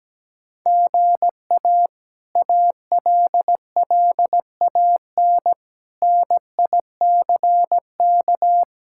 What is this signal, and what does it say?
Morse code, heard as a keyed tone, about 17 wpm, 700 Hz. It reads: GA ALLAN NICK
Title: K1USN SST QSO Practice - 17wpm